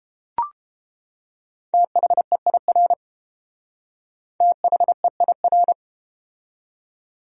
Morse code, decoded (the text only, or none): THEIR THEIR